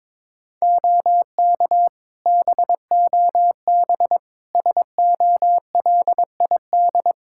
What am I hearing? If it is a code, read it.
Morse code, heard as a keyed tone, about 21 wpm, 700 Hz. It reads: OK BOB SOLID
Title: Practice QSO - Exchange 3 - More Info - 22wpm